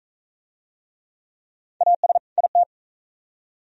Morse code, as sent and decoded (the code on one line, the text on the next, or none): .- ... / .. -
AS IT